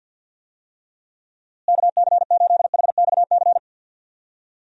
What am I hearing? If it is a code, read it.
Morse code, heard as a keyed tone, about 47 wpm, 700 Hz. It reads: KC8HXC
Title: USA Call Signs - 6 Characters - Repeated 2x - 50wpm